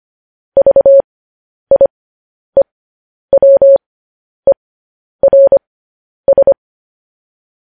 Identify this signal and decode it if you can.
Morse code, heard as a keyed tone, about 25 wpm, spaced as for 10 wpm, 550 Hz. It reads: VIEWERS
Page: news-headlines-in-morse-code-at-10-wpm